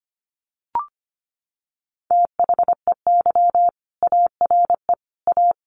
Morse code, heard as a keyed tone, about 25 wpm, 700 Hz. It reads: THEY ARE A
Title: Sentences from Top 1200 Words 25wpm